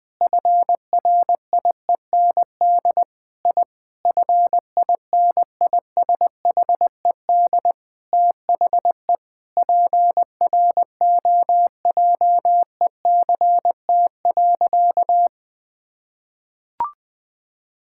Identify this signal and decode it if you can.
Morse code, heard as a keyed tone, about 20 wpm, 700 Hz. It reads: FRIEND I FINISHED THE PROJECT.
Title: Sentences with 5-Word Phrases - Spoken Beginning - 20wpm